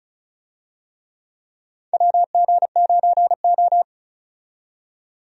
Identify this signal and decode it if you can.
Morse code, heard as a keyed tone, about 36 wpm, 700 Hz. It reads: WG9O
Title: USA Call Signs - 4 Characters - 35wpm